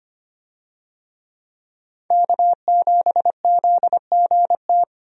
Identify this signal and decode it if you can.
Morse code, heard as a keyed tone, about 25 wpm, 700 Hz. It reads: K7ZGT
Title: USA Call Signs - 5 Characters - 25wpm